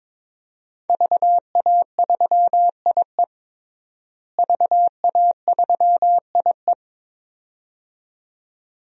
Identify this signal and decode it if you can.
Morse code, heard as a keyed tone, about 21 wpm, 700 Hz. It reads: VA3IE VA3IE
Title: DXCC-100 Call Signs - 5 Characters - Repeated 2x - 22wpm